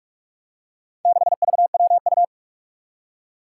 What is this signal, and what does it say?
Morse code, heard as a keyed tone, about 44 wpm, 700 Hz. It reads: 6VWU
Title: Four Challenging Letter-Number Combinations 45wpm